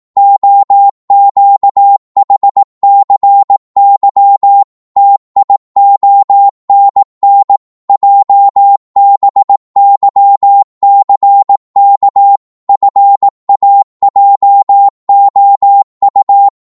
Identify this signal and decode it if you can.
Morse code, heard as a keyed tone, about 18 wpm, 800 Hz. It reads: OQHCY TIONN JBYCK FAJOU
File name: Morse / ltr-18wpm